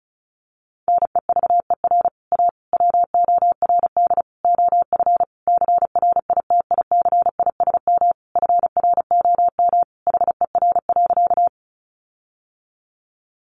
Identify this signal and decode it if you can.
Morse code, heard as a keyed tone, about 34 wpm, 700 Hz. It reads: NEVER A WORD OF CRITICISM FROM HER.
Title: Sentences from Top 1300 Words 35wpm